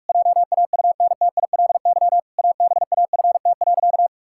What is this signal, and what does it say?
Morse code, heard as a keyed tone, about 48 wpm, 700 Hz. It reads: JAUNTILY ABAFT.
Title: Pangram Challenge v2 - 45wpm